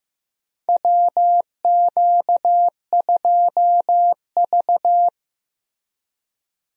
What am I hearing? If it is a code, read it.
Morse code, heard as a keyed tone, about 16 wpm, 700 Hz. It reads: WQ2V
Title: USA Call Signs - 4 Characters - Repeated 2x - 15wpm